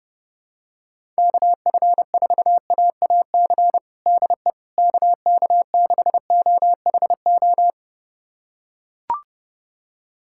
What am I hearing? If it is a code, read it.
Morse code, heard as a keyed tone, about 32 wpm, 700 Hz. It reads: KF4AAC DE KK6OHO
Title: USA Callsign DE Callsign 30wpm